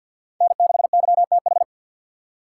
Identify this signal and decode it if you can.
Morse code, heard as a keyed tone, about 53 wpm, 700 Hz. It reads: N6YTH
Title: USA Call Signs - 5 Characters - 6x Character Spacing - 50wpm